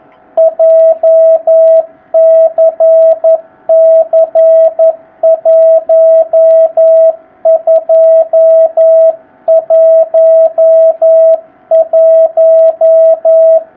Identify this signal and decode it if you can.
Morse code, heard as a keyed tone, about 11 wpm, 650 Hz. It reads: JCC1211
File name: jh8cbh / 2019cw